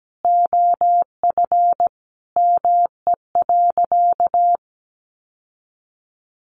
Morse code, heard as a keyed tone, about 17 wpm, 700 Hz. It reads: OF ME.
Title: Sentences from Top 1300 Words 17wpm